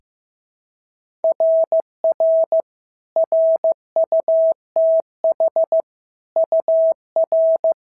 Morse code, heard as a keyed tone, about 14 wpm, 650 Hz. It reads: RR RUTH UR
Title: SOTA QSO Practice 15wpm